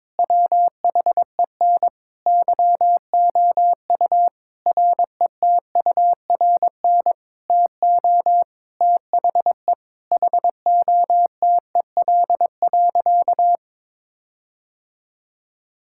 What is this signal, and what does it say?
Morse code, heard as a keyed tone, about 22 wpm, 700 Hz. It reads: WHEN YOU RETURN TO THE HOTEL.
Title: Sentences from Top 1600 Words 22wpm